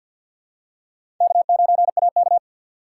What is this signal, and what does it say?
Morse code, heard as a keyed tone, about 48 wpm, 700 Hz. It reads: K9AK
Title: USA Call Signs - 3x Character Spacing - 50wpm